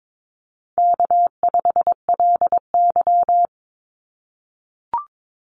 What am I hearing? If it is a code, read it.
Morse code, heard as a keyed tone, about 22 wpm, 700 Hz. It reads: K5LY
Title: Easy Call Signs - 22wpm